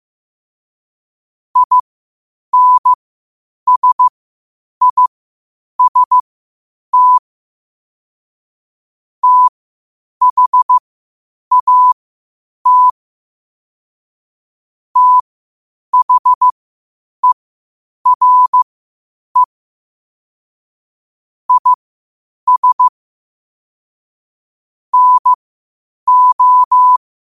Morse code, heard as a keyed tone, about 15 wpm, spaced as for 8 wpm, 1000 Hz. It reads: INSIST THAT THERE IS NO
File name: qotdfiles / 8wpm